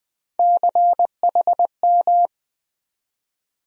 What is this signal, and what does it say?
Morse code, heard as a keyed tone, about 20 wpm, 700 Hz. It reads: CHM
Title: USA Call Sign Postfixes - 3 Letters 20wpm